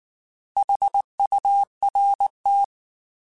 Morse code, heard as a keyed tone, about 19 wpm, 800 Hz. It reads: HURT